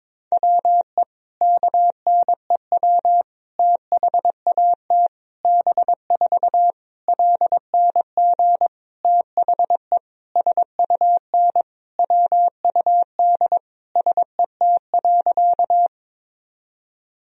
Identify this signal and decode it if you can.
Morse code, heard as a keyed tone, about 22 wpm, 700 Hz. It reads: WE KNEW THAT B4 LNG THE SUN WUD SET.
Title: Sentences from Top 400 Words and Abbreviations 22wpm